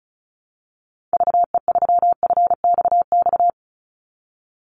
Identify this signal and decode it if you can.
Morse code, heard as a keyed tone, about 34 wpm, 700 Hz. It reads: VE3FXX
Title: Call Signs - 35wpm